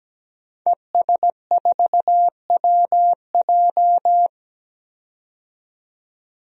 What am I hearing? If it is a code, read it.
Morse code, heard as a keyed tone, about 17 wpm, 700 Hz. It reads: ES4WJ
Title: DXCC-100 Call Signs - 5 Characters - 17wpm